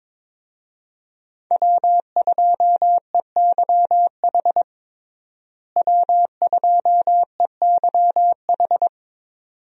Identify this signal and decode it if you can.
Morse code, heard as a keyed tone, about 22 wpm, 700 Hz. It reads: W2EYH W2EYH
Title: USA Call Signs - 5 Characters - Repeated 2x - 22wpm